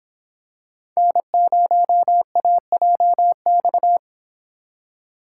Morse code, heard as a keyed tone, about 26 wpm, 700 Hz. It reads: N0AJX